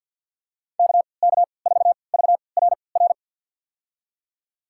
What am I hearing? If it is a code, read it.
Morse code, heard as a keyed tone, about 52 wpm, spaced as for 28 wpm, 700 Hz. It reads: KK4VRR
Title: USA Call Signs - 6 Characters - 3x Character Spacing - 50wpm